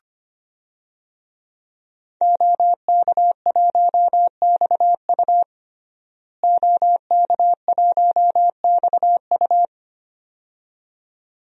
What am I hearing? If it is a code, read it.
Morse code, heard as a keyed tone, about 25 wpm, 700 Hz. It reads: OK1XU OK1XU
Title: DXCC-100 Call Signs - 5 Characters - Repeated 2x - 25wpm